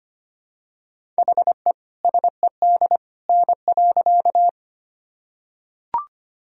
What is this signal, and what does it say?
Morse code, heard as a keyed tone, about 25 wpm, 700 Hz. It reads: HE SED N.
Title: Sentences from Top 100 Words and Abbreviations 25wpm